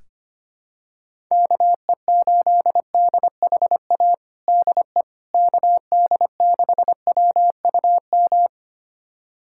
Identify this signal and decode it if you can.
Morse code, heard as a keyed tone, about 25 wpm, 700 Hz. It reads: KE8DHA DE KD6WUM
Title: USA Callsign DE Callsign - 6 Characters - 25wpm